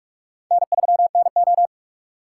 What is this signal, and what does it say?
Morse code, heard as a keyed tone, about 46 wpm, 700 Hz. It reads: N3NO
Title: USA Call Signs - 4 Characters - 4x Character Spacing - 45wpm